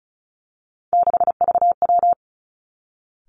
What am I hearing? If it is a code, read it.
Morse code, heard as a keyed tone, about 36 wpm, 700 Hz. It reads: 6VW
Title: Three Challenging Letter-Number Combinations 35wpm